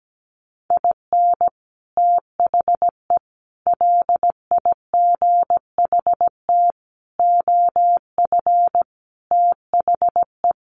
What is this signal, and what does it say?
Morse code, heard as a keyed tone, about 17 wpm, 700 Hz. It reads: IN THE LIGHT OF THE
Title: Sets of 5 Words Encore2 17wpm